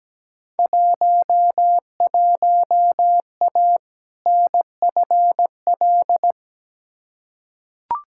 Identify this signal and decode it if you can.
Morse code, heard as a keyed tone, about 17 wpm, 700 Hz. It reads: 11A NFL
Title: Field Day Exchanges Encore - 17wpm